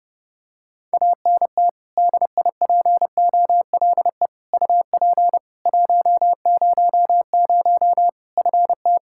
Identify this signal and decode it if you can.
Morse code, heard as a keyed tone, about 32 wpm, 700 Hz. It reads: ANT DIPOLE UP 100 FT
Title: Practice QSO - Exchange 2 - Info - 30wpm